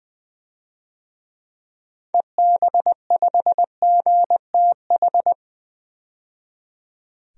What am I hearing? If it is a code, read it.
Morse code, heard as a keyed tone, about 21 wpm, 700 Hz. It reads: EB5GTH
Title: DXCC-100 Call Signs - 6 Characters - 20wpm